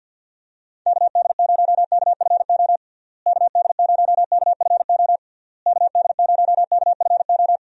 Morse code, heard as a keyed tone, about 50 wpm, 700 Hz. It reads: KD0KFO KD0KFO KD0KFO
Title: USA Call Signs - 6 Characters - Repeated 3x - 50wpm